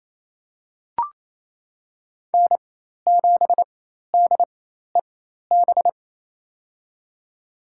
Morse code, heard as a keyed tone, about 28 wpm, spaced as for 13 wpm, 700 Hz. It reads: N7DEB
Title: USA Call Signs - 5 Characters - 4x Character Spacing - 28wpm